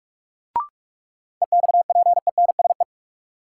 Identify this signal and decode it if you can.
Morse code, heard as a keyed tone, about 41 wpm, 700 Hz. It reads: EXPENSE
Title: Sets of 4 Words - New Word Drill - 45wpm